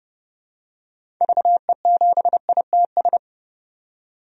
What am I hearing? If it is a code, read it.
Morse code, heard as a keyed tone, about 30 wpm, 700 Hz. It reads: VE7ITS
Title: DXCC-100 Call Signs - 6 Characters - 30wpm